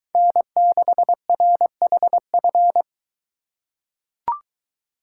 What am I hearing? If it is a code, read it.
Morse code, heard as a keyed tone, about 23 wpm, 700 Hz. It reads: N6RHF